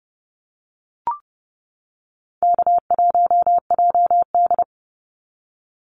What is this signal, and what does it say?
Morse code, heard as a keyed tone, about 30 wpm, 700 Hz. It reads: K1JD
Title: NA SOTA Activator Callsigns 30wpm